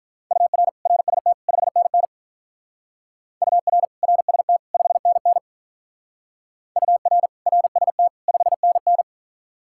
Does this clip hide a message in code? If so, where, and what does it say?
Yes: Morse code from 0.3 to 9.0 s, heard as a keyed tone, about 46 wpm, 700 Hz. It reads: UR RST 5NN UR RST 5NN UR RST 5NN